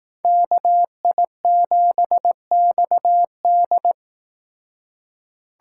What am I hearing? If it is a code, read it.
Morse code, heard as a keyed tone, about 18 wpm, 700 Hz. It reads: KI7XD
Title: Speed Racing - USA Call Signs 15wpm